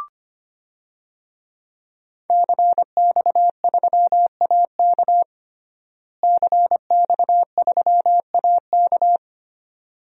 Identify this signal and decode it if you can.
Morse code, heard as a keyed tone, about 25 wpm, 700 Hz. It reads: CX3AK CX3AK
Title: DXCC-100 Call Signs - 5 Characters - Repeated 2x - 25wpm